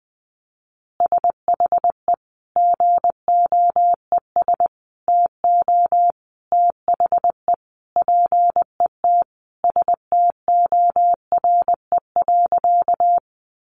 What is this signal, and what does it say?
Morse code, heard as a keyed tone, about 20 wpm, 700 Hz. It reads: SHE GOES TO THE PET STORE.
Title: Easy Phrases 20wpm